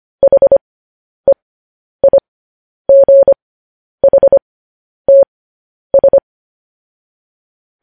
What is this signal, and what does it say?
Morse code, heard as a keyed tone, about 25 wpm, spaced as for 10 wpm, 550 Hz. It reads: HEIGHTS
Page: news headlines in morse code at 10 wpm